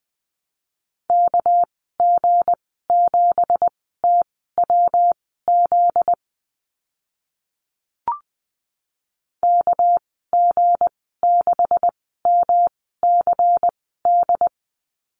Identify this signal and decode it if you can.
Morse code, heard as a keyed tone, about 20 wpm, spaced as for 14 wpm, 700 Hz. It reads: KG7TWZ KG6MCD
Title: 245 Lesson 39g - Letter z - Call Signs 20wpm